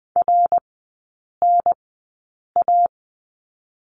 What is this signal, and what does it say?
Morse code, heard as a keyed tone, about 20 wpm, spaced as for 8 wpm, 700 Hz. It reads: RNA
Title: 081 Lesson 14e - Number 2 - Three Characters 20wpm